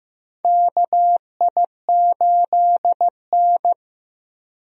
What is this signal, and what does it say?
Morse code, heard as a keyed tone, about 15 wpm, 700 Hz. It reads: KI8N
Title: USA Call Signs - 4 Characters - Repeated 3x - 15wpm